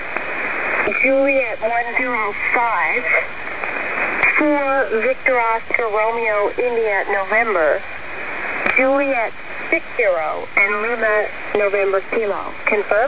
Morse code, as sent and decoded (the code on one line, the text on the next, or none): none